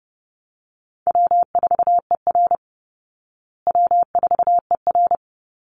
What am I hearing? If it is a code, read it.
Morse code, heard as a keyed tone, about 30 wpm, 700 Hz. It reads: W4ER W4ER
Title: Easy Call Signs Repeated 2x - 30wpm